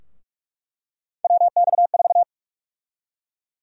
Morse code, heard as a keyed tone, about 46 wpm, 700 Hz. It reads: WX4
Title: USA Call Sign Prefixes - 45wpm